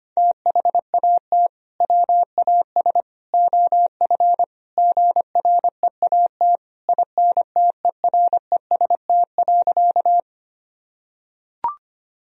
Morse code, heard as a keyed tone, about 25 wpm, 700 Hz. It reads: THAT WAS OF GREAT INTEREST.
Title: Sentences from Top 1000 Words - Review Part 2 - 25wpm